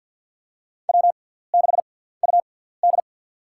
Morse code, heard as a keyed tone, about 53 wpm, spaced as for 17 wpm, 700 Hz. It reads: W6UD
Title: USA Call Signs - 6x Character Spacing - 50wpm